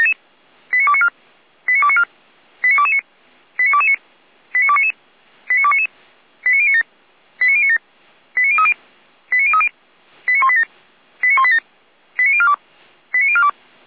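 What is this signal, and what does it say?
Five-tone selective call (ZVEI), heard as quick runs of five tones, tones of about 70 ms. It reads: 89294 89294 89229 89229 89299 89299 89097 89097 89030 89030 89187 89187 89042 89042
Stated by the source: Five-tone ZVEI call 89276 received on an SDR, from feuerwehr-orlamuende.org (Feuerwehr Orlamünde fire brigade).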